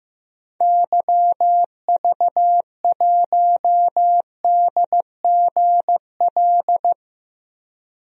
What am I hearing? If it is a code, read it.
Morse code, heard as a keyed tone, about 15 wpm, 700 Hz. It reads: YV1DGL
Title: DXCC-100 Call Signs - 6 Characters - Repeated 3x - 15wpm